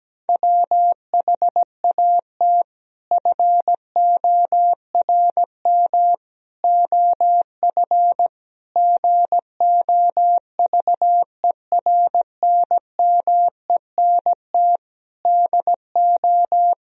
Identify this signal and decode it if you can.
Morse code, heard as a keyed tone, about 17 wpm, 700 Hz. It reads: WHAT FORM OF GOVERNMENT DO
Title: Sentences from Top 1000 Words - Review Part 3 - 17wpm